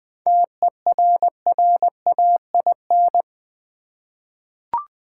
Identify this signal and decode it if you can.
Morse code, heard as a keyed tone, about 20 wpm, 700 Hz. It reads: TERRAIN
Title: 7 Letter Words - First 3 Letters Given - 20wpm